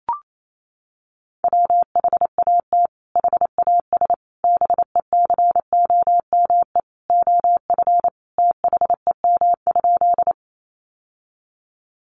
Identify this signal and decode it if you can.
Morse code, heard as a keyed tone, about 28 wpm, 700 Hz. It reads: WHAT HAS BECOME OF THEM?
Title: Sentences from Top 600 Words 28wpm